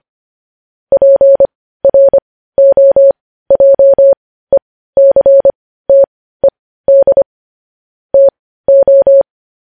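Morse code, heard as a keyed tone, about 25 wpm, spaced as for 15 wpm, 550 Hz. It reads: PROJECTED TO
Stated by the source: News headlines sent as Morse practice at 15.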